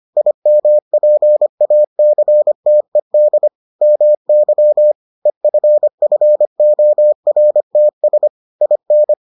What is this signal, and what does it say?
Morse code, heard as a keyed tone, about 25 wpm, 600 Hz. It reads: IMPACTED MY EFFORTS IN